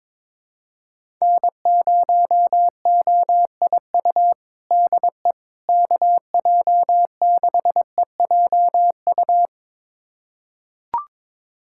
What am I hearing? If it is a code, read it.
Morse code, heard as a keyed tone, about 22 wpm, 700 Hz. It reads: N0OIU DE KJ6EJU
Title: USA Callsign DE Callsign 22wpm